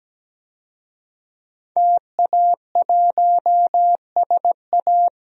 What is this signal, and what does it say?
Morse code, heard as a keyed tone, about 17 wpm, 700 Hz. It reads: TA1SA
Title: DXCC-100 Call Signs - 5 Characters - Repeated 2x - 17wpm